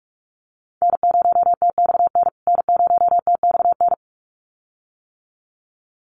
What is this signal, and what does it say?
Morse code, heard as a keyed tone, about 44 wpm, 700 Hz. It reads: N0TXN N0TXN
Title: USA Call Signs Repeated 2x 45wpm